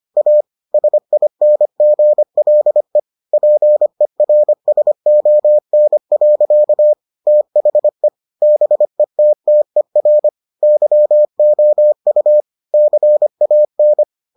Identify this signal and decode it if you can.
Morse code, heard as a keyed tone, about 25 wpm, 600 Hz. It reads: A SINGLE PERSON. THE BETTER YOU CAN